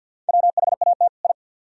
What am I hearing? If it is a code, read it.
Morse code, heard as a keyed tone, about 62 wpm, 700 Hz. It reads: WHAT I
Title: Speed-Racing - Sets of 2 Words 50wpm